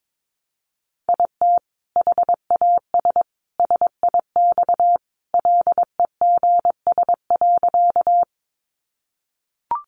Morse code, heard as a keyed tone, about 22 wpm, 700 Hz. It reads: IT HAS SIX LEGS.
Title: Additional Easy Phrases 22wpm